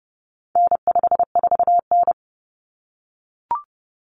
Morse code, heard as a keyed tone, about 30 wpm, 700 Hz. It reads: N54N
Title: Common Cut Numbers - 4-Digit 30wpm